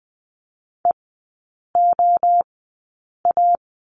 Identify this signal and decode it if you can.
Morse code, heard as a keyed tone, about 20 wpm, 700 Hz. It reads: E O A
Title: 029 Lesson 6e - Letter i - Three Characters 20wpm